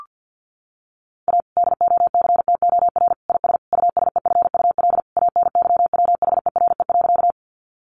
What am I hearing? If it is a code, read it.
Morse code, heard as a keyed tone, about 50 wpm, 700 Hz. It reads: A DOCTOR IS USEFUL ANYWHERE.
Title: Sentences from Top 1500 Words 50wpm